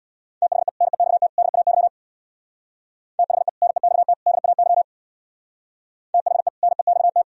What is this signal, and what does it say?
Morse code, heard as a keyed tone, about 60 wpm, 700 Hz. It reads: THE NEXT DAY THE NEXT DAY THE NEXT
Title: Speed-Racing - Sets of 3 Words 50wpm